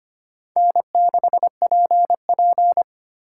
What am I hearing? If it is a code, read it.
Morse code, heard as a keyed tone, about 25 wpm, 700 Hz. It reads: N6PP
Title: USA Call Signs - 4x Character Spacing - 25wpm